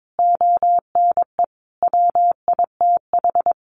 Morse code, heard as a keyed tone, about 22 wpm, 700 Hz. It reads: ONE WITH
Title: Sentences from Top 500 Words 22wpm